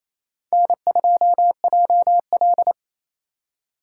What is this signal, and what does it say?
Morse code, heard as a keyed tone, about 35 wpm, 700 Hz. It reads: N2JL
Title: Speed Racing - USA Call Signs 28wpm